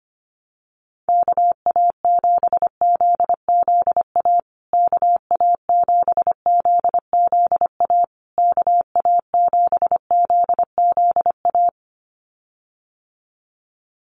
Morse code, heard as a keyed tone, about 25 wpm, 700 Hz. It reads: KA7ZZA KA7ZZA KA7ZZA
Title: USA Call Signs Repeated 3x 25wpm